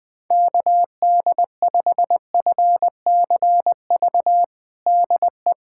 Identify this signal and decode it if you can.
Morse code, heard as a keyed tone, about 20 wpm, 700 Hz. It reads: KD5FCV DE
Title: USA Callsign DE Callsign 20wpm